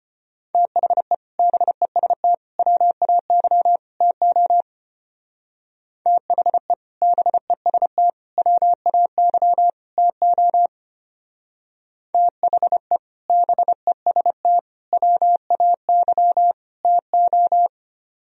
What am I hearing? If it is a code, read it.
Morse code, heard as a keyed tone, about 30 wpm, 700 Hz. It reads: THE BEST WAY TO THE BEST WAY TO THE BEST WAY TO
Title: Speed-Racing - Sets of 4 Words 25wpm